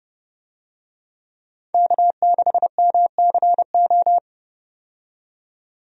Morse code, heard as a keyed tone, about 32 wpm, 700 Hz. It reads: K6MCO
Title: USA Call Signs - 5 Characters - 30wpm